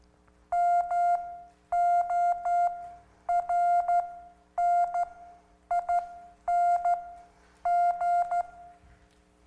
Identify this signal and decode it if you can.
Morse code, heard as a keyed tone, about 13 wpm, spaced as for 9 wpm, 700 Hz. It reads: MORNING